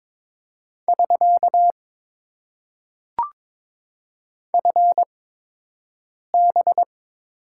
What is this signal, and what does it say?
Morse code, heard as a keyed tone, about 22 wpm, spaced as for 6 wpm, 700 Hz. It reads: <SK> FB
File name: Top 90 QSO Elements - 8x Character Spacing - 22wpm